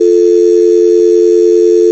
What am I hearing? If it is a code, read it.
Dial tone (North America), 350 + 440 Hz.
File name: force-audio